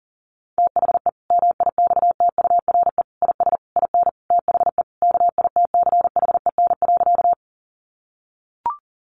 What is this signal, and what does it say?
Morse code, heard as a keyed tone, about 37 wpm, 700 Hz. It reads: THE MIXTURE IS IN THE KITCHEN.
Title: Sentences from Top 1500 Words 40wpm